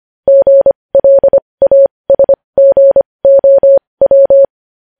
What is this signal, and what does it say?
Morse code, heard as a keyed tone, about 25 wpm, spaced as for 20 wpm, 550 Hz. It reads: GLASGOW
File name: podcast / News headlines-1x-25x20